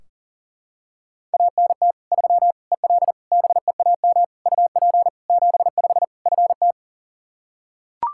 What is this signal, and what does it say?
Morse code, heard as a keyed tone, about 40 wpm, 700 Hz. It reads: ANT 3 EL BEAM UP 75 FT
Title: Practice QSO - Exchange 2 - Info - 40wpm